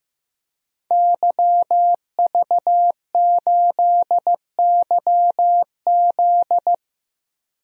Morse code, heard as a keyed tone, about 15 wpm, 700 Hz. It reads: YV8YZ
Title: DXCC-100 Call Signs - 5 Characters - 15wpm